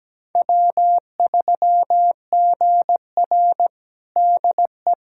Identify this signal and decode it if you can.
Morse code, heard as a keyed tone, about 17 wpm, 700 Hz. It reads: W3GR DE
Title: USA Callsign DE Callsign - 4 Characters - 17wpm